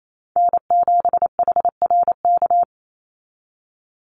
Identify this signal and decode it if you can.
Morse code, heard as a keyed tone, about 28 wpm, 700 Hz. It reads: N7HRK
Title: Call Signs Repeated 3x - 28wpm